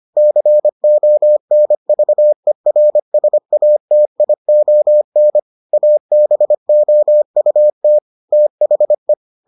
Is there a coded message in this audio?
Yes: Morse code from 0.2 to 9.1 s, heard as a keyed tone, about 25 wpm, 600 Hz. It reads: CONVERSATION ABOUT THE